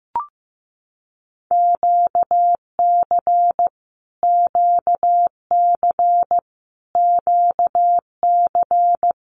Morse code, heard as a keyed tone, about 15 wpm, 700 Hz. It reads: QC QC QC
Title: Canadian Provinces and Territories - Repeated 3x - 15wpm